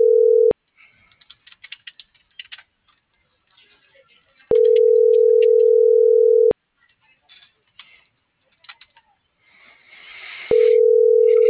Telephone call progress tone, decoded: ringback tone

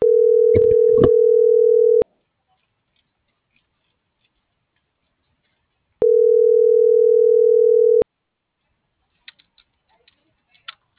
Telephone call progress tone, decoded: ringback tone

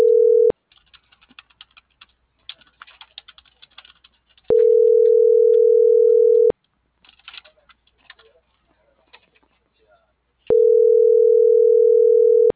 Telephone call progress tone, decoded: ringback tone

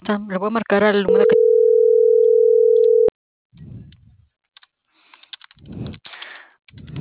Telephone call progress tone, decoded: ringback tone

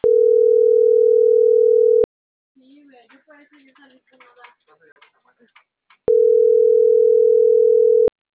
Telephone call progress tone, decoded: ringback tone

